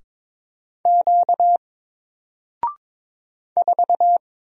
Morse code, heard as a keyed tone, about 22 wpm, 700 Hz. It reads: Q 4
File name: Single-Letter-Number-22wpm